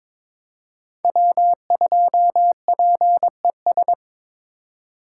Morse code, heard as a keyed tone, about 26 wpm, 700 Hz. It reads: W2PES
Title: Speed Racing - USA 1x3 Call Signs 22wpm